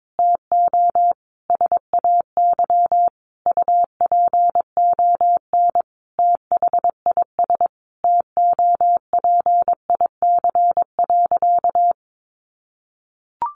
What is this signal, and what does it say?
Morse code, heard as a keyed tone, about 22 wpm, 700 Hz. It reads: TO SAY UPON THIS TOPIC.
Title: Sentences from Top 1200 Words 22wpm